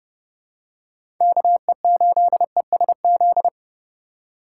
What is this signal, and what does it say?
Morse code, heard as a keyed tone, about 30 wpm, 700 Hz. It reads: KE8ESZ
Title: USA Call Signs - 6 Characters - 2x Character Spacing - 30wpm